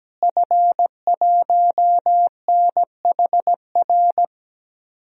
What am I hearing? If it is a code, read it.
Morse code, heard as a keyed tone, about 17 wpm, 700 Hz. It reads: F1NHR
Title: DXCC-100 Call Signs - 5 Characters - Repeated 2x - 17wpm